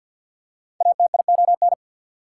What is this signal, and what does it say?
Morse code, heard as a keyed tone, about 54 wpm, 690 Hz. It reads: ATION
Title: Prefixes and Suffixes 50wpm